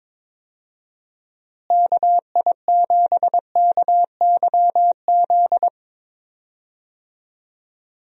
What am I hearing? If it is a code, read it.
Morse code, heard as a keyed tone, about 21 wpm, 700 Hz. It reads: KI7KYZ
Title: USA Call Signs - 6 Characters - Repeated 2x - 22wpm